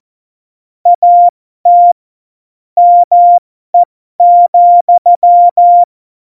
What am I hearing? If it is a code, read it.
Morse code, heard as a keyed tone, about 14 wpm, spaced as for 12 wpm, 700 Hz. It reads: AT ME,
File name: Morsekurs / Witze